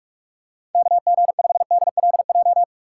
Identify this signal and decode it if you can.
Morse code, heard as a keyed tone, about 46 wpm, 700 Hz. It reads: KG5DLJ
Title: USA Call Signs - 6 Characters - 4x Character Spacing - 45wpm